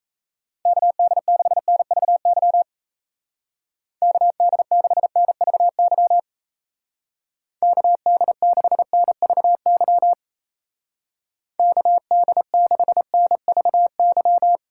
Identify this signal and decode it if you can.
Morse code, heard as a keyed tone, about 35 wpm, 700 Hz. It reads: KD6NVY KD6NVY KD6NVY KD6NVY
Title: Speed Racing - USA 2x3 Call Signs 28wpm